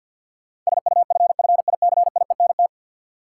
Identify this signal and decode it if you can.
Morse code, heard as a keyed tone, about 52 wpm, 700 Hz. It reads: SUFFICIENT
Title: Sentences from Top 1400 Words - New Word Drill Alt - 50wpm